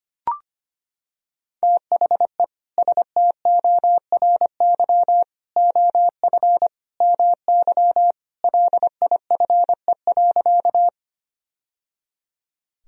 Morse code, heard as a keyed tone, about 25 wpm, 700 Hz. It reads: THE STORY OF MY LIFE.
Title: Sentences from Top 1000 Words - Review Part 1 - 25wpm